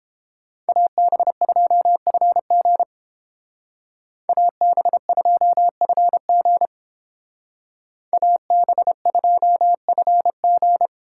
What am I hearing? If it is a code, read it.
Morse code, heard as a keyed tone, about 30 wpm, 700 Hz. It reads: AB2FG AB2FG AB2FG